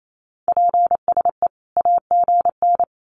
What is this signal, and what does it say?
Morse code, heard as a keyed tone, about 28 wpm, 700 Hz. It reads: PSE AGN
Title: Common QSO Phrases 28wpm